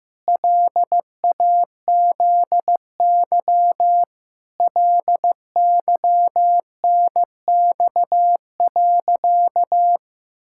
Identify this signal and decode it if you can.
Morse code, heard as a keyed tone, about 15 wpm, 700 Hz. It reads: LAZY LYNX.
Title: Pangram Challenge v2 - 15wpm